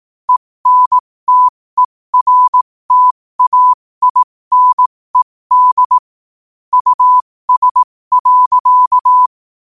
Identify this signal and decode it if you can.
Morse code, heard as a keyed tone, about 18 wpm, spaced as for 15 wpm, 1000 Hz. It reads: ENTERTAINED US.